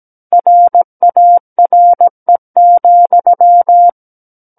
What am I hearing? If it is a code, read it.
Morse code, heard as a keyed tone, about 17 wpm, 700 Hz. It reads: RARE,
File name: Book One / 08-Chapter 8 - 17 WPM